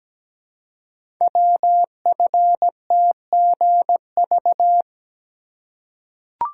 Morse code, heard as a keyed tone, about 17 wpm, 700 Hz. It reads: WFTGV
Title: Code Groups - 5 Letters - 17wpm